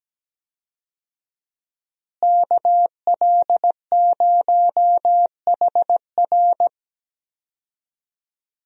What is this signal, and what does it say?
Morse code, heard as a keyed tone, about 17 wpm, 700 Hz. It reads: KL0HR
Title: USA Call Signs - 5 Characters - Repeated 3x - 17wpm